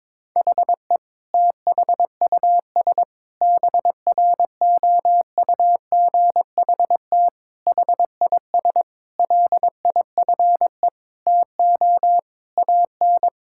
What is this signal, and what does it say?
Morse code, heard as a keyed tone, about 22 wpm, 700 Hz. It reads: HE THUS BROUGHT HIS LIFE TO AN
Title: Sentences from Top 1000 Words - Review Part 5 - 22wpm